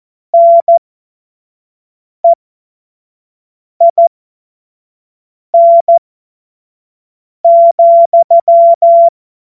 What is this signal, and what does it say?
Morse code, heard as a keyed tone, about 14 wpm, spaced as for 5 wpm, 680 Hz. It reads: NEIN,
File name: Morsekurs / Witze / Witz 04. Mai 2021, 25 F70